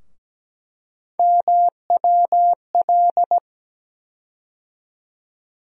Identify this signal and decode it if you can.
Morse code, heard as a keyed tone, about 17 wpm, 700 Hz. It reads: MWL
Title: USA Call Sign Postfixes - 3 Letters 17wpm